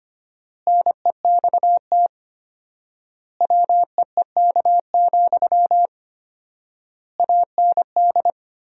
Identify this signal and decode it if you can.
Morse code, heard as a keyed tone, about 25 wpm, 700 Hz. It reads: NEXT WEEK, AND
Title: 4x Word-Spacing - Sentences from Top 500 Words - 25wpm